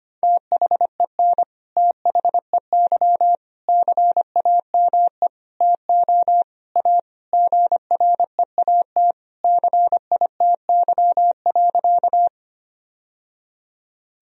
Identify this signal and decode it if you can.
Morse code, heard as a keyed tone, about 25 wpm, 700 Hz. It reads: THEN THEY CAME TO A GREAT CITY.
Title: Sentences from Top 1000 Words - Review Part 1 - 25wpm